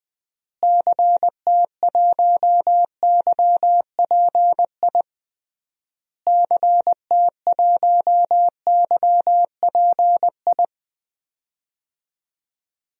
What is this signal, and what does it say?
Morse code, heard as a keyed tone, about 20 wpm, 700 Hz. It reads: CT1YPI CT1YPI
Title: DXCC-100 Call Signs - 6 Characters - Repeated 3x - 20wpm